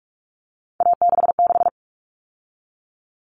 Morse code, heard as a keyed tone, about 44 wpm, 700 Hz. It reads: A66
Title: All Cut Numbers - 3-Digit 45wpm